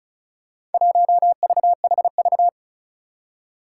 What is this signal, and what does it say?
Morse code, heard as a keyed tone, about 36 wpm, 700 Hz. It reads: 1VHV
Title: Four Challenging Letter-Number Combinations 35wpm